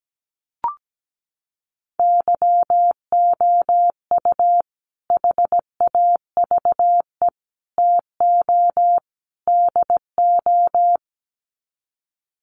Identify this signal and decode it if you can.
Morse code, heard as a keyed tone, about 17 wpm, 700 Hz. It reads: YOU HAVE TO DO
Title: Sets of 4 Words 17wpm